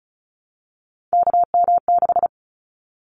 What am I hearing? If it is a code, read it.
Morse code, heard as a keyed tone, about 34 wpm, 700 Hz. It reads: KM6
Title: Callsign Prefixes for Top 25 DXCC - Repeated 3x - 35wpm